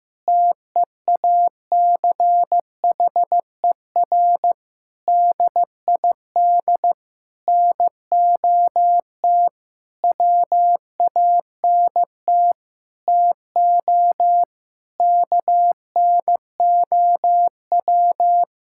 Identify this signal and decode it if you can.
Morse code, heard as a keyed tone, about 15 wpm, 700 Hz. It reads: TEACHER DID NOT WANT TO KNOW
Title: Sentences from Top 1000 Words - Review Part 6 - 15wpm